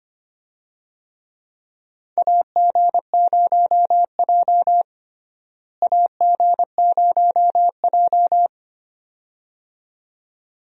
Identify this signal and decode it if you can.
Morse code, heard as a keyed tone, about 25 wpm, 700 Hz. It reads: AG0J AG0J
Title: USA Call Signs - 4 Characters - Repeated 2x - 25wpm